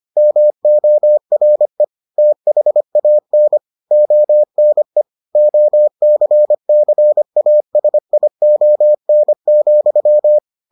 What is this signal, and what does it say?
Morse code, heard as a keyed tone, about 25 wpm, 600 Hz. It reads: MORE THAN ONE OCCASION,